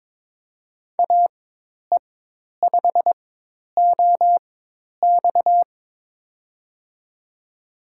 Morse code, heard as a keyed tone, about 22 wpm, spaced as for 10 wpm, 700 Hz. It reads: AE5OX
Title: USA Call Signs - 5 Characters - 4x Character Spacing - 22wpm